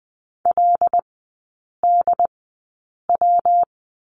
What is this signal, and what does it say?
Morse code, heard as a keyed tone, about 20 wpm, spaced as for 8 wpm, 700 Hz. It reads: LDW
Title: 114 Lesson 19e - Letter w - Three Characters 20wpm